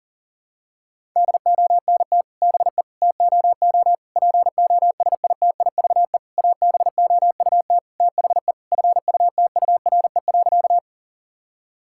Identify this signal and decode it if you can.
Morse code, heard as a keyed tone, about 37 wpm, 700 Hz. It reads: DONT BE TOO POSITIVE ABOUT THE FUTURE.
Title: Sentences from Top 1000 Words - Review Part 5 - 40wpm